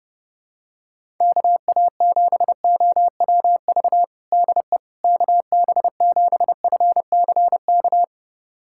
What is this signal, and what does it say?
Morse code, heard as a keyed tone, about 30 wpm, 700 Hz. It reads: KA7OWV DE KB7FCK
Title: USA Callsign DE Callsign - 6 Characters - 30wpm